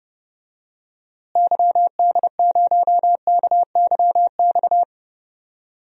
Morse code, heard as a keyed tone, about 30 wpm, 700 Hz. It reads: YD0KYX
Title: DXCC-100 Call Signs - 6 Characters - Repeated 2x - 30wpm